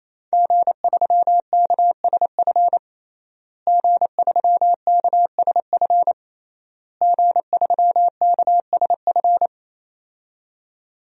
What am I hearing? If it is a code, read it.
Morse code, heard as a keyed tone, about 28 wpm, 700 Hz. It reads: G3KSF G3KSF G3KSF
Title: DXCC-100 Call Signs - 5 Characters - Repeated 3x - 28wpm